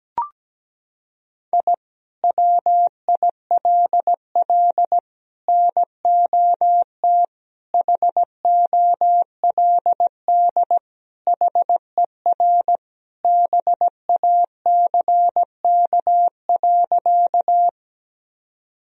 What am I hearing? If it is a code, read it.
Morse code, heard as a keyed tone, about 17 wpm, 700 Hz. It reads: I WILL NOT HOLD HER BACK.
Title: Sentences from Top 1000 Words - Review Part 2 - 17wpm